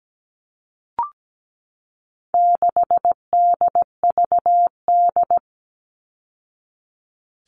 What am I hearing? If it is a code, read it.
Morse code, heard as a keyed tone, about 17 wpm, 700 Hz. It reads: 6DVD
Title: All Cut Numbers - 4-Digit 17wpm